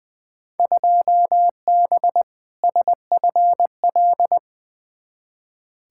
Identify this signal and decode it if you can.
Morse code, heard as a keyed tone, about 20 wpm, 700 Hz. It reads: 2B SFL
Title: Field Day Exchanges - 20wpm